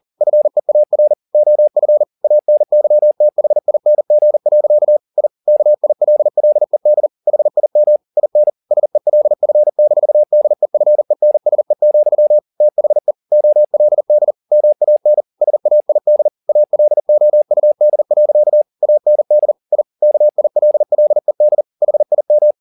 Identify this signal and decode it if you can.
Morse code, heard as a keyed tone, about 40 wpm, 600 Hz. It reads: FEAR OF ANYTHING. I KILLED HIM IN SELF-DEFENSE, THE OLD MAN SAID ALOUD. AND I KILLED HIM
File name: the old man and the sea 2 40wpm